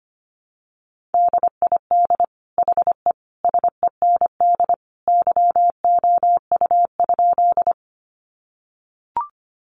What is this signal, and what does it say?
Morse code, heard as a keyed tone, about 25 wpm, 700 Hz. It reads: DID HE SEND YOU?
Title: Sentences from Top 1500 Words 25wpm